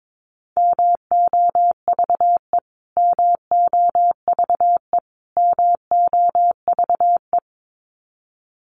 Morse code, heard as a keyed tone, about 22 wpm, 700 Hz. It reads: MOVE MOVE MOVE
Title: Top 200 Words Repeated 22wpm